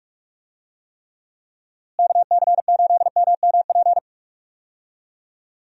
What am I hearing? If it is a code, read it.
Morse code, heard as a keyed tone, about 42 wpm, 700 Hz. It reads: KC8MMP
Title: USA Call Signs - 6 Characters - Repeated 3x - 45wpm